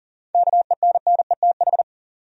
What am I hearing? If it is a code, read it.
Morse code, heard as a keyed tone, about 40 wpm, 700 Hz. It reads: KENNETH
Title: USA Common Names - 40wpm